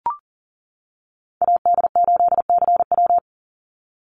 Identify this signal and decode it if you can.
Morse code, heard as a keyed tone, about 40 wpm, 700 Hz. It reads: AD8CW
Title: Call Signs - 40wpm